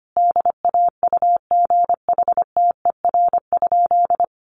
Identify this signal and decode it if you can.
Morse code, heard as a keyed tone, about 25 wpm, 700 Hz. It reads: DAUGHTER?
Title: Sentences from Top 1500 Words 25wpm